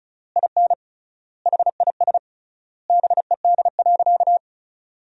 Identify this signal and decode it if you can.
Morse code, heard as a keyed tone, about 36 wpm, 700 Hz. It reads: IN HIS BED.
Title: 3x Word-Spacing - Sentences from Top 500 Words - 35wpm